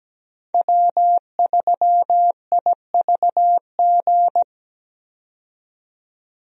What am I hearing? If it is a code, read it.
Morse code, heard as a keyed tone, about 17 wpm, 700 Hz. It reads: W3IVG
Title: USA Call Signs - 5 Characters - 17wpm